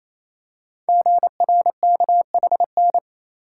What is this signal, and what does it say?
Morse code, heard as a keyed tone, about 27 wpm, 700 Hz. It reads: GRKHN
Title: Code Groups - 5 Letter:Numbers - 28wpm